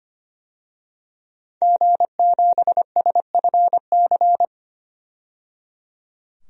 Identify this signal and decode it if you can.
Morse code, heard as a keyed tone, about 25 wpm, 700 Hz. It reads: G7SFC
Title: DXCC-100 Call Signs - 5 Characters - 25wpm